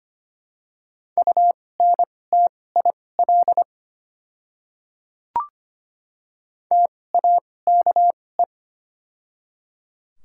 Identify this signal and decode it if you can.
Morse code, heard as a keyed tone, about 25 wpm, spaced as for 18 wpm, 700 Hz. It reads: UNTIL TAKE
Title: 2x Character Spacing - Top 300 Words - 25wpm